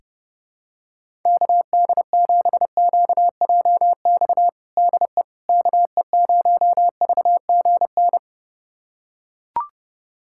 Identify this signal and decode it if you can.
Morse code, heard as a keyed tone, about 32 wpm, 700 Hz. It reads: KD7QJX DE KE0VGN
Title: USA Callsign DE Callsign 30wpm